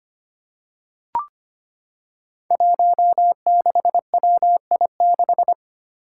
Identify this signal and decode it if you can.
Morse code, heard as a keyed tone, about 25 wpm, 700 Hz. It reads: 16WI6
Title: Five Challenging Letter-Number Combinations 25wpm